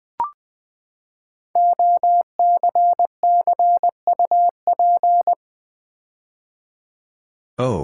Morse code, heard as a keyed tone, about 20 wpm, 700 Hz. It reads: OCCUP
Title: Build A Word - 5 Letters At A Time - 10-Letter Words - 20wpm